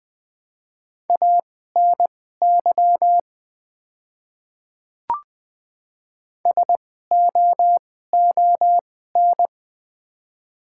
Morse code, heard as a keyed tone, about 20 wpm, spaced as for 14 wpm, 700 Hz. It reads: ANY SOON
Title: 2x Character Spacing - Top 500 Words - 20wpm